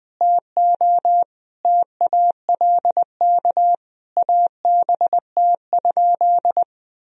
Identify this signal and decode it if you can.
Morse code, heard as a keyed tone, about 21 wpm, 700 Hz. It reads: TO TALK ABT?
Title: Sentences from Top 400 Words and Abbreviations 20wpm